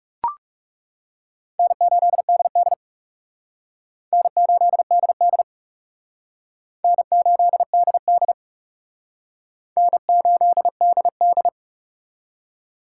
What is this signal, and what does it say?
Morse code, heard as a keyed tone, about 38 wpm, 700 Hz. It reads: N8DD N8DD N8DD N8DD